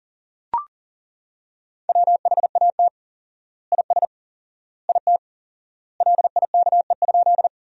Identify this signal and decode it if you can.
Morse code, heard as a keyed tone, about 40 wpm, 700 Hz. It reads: WHAT IS IT LIKE?
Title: 4x Word-Spacing - Sentences from Top 100 Words - 40wpm